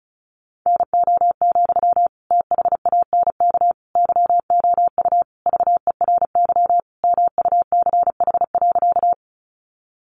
Morse code, heard as a keyed tone, about 34 wpm, 700 Hz. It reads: NO, THANK YOU VERY MUCH.
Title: Sentences from Top 600 Words 35wpm